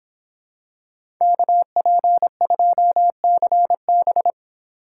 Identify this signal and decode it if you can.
Morse code, heard as a keyed tone, about 26 wpm, 700 Hz. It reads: KP2CB